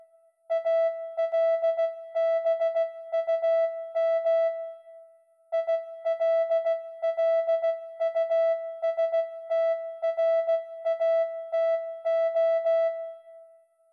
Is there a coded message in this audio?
Yes: Morse code from 0.5 to 12.9 s, heard as a keyed tone, about 16 wpm, 660 Hz. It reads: ALBUM ILLUSTRATO